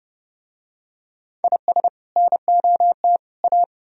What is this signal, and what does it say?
Morse code, heard as a keyed tone, about 28 wpm, 700 Hz. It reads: IS NOT A